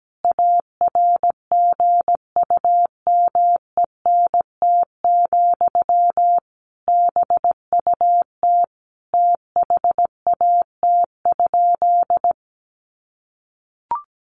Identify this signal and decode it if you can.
Morse code, heard as a keyed tone, about 17 wpm, 700 Hz. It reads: ARGUMENT, BUT THAT?
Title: Sentences from Top 1400 Words 17wpm